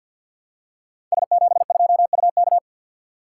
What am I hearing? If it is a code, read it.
Morse code, heard as a keyed tone, about 48 wpm, 700 Hz. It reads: S72UK
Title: Code Groups - 5 Letter:Numbers - 50wpm